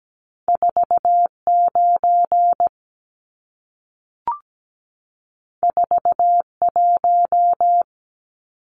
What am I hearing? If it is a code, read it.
Morse code, heard as a keyed tone, about 17 wpm, 700 Hz. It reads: 49 41
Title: Numbers - 2-Digit 17wpm